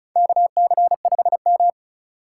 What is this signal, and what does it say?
Morse code, heard as a keyed tone, about 36 wpm, 700 Hz. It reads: KC5M
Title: USA Call Signs - 4x Character Spacing - 35wpm